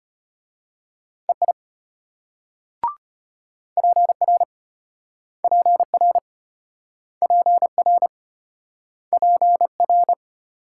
Morse code, heard as a keyed tone, about 31 wpm, 700 Hz. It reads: EI PR PR PR PR
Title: Speed-Racing Multi-letter Phonemes - 25wpm